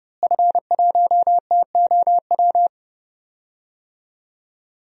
Morse code, heard as a keyed tone, about 30 wpm, 700 Hz. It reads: F1TOW
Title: DXCC-100 Call Signs - 5 Characters - Repeated 2x - 30wpm